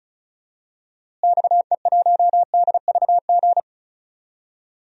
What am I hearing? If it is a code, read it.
Morse code, heard as a keyed tone, about 34 wpm, 700 Hz. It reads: XE1DVG
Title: DXCC-100 Call Signs - 6 Characters - Repeated 3x - 35wpm